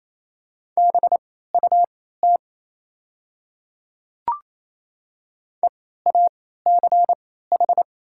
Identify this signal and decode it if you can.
Morse code, heard as a keyed tone, about 28 wpm, spaced as for 16 wpm, 700 Hz. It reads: BUT EACH
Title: 3x Character Spacing - Top 200 Words - 28wpm